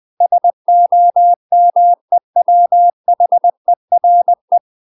Morse code, heard as a keyed tone, about 20 wpm, 700 Hz. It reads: SOMEWHERE
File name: UK MASTER QSOs TRAINING / 20WPM / UK MASTER QSOs 0030